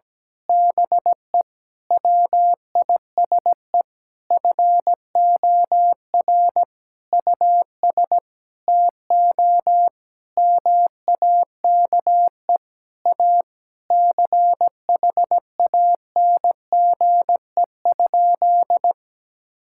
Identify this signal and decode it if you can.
Morse code, heard as a keyed tone, about 17 wpm, 700 Hz. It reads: BE WISE FOR US TO MAKE A CHANGE?
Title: Sentences from Top 1600 Words 17wpm